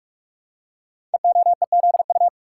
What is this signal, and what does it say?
Morse code, heard as a keyed tone, about 44 wpm, 700 Hz. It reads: EOEZU
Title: Code Groups - 5 Letters - 45wpm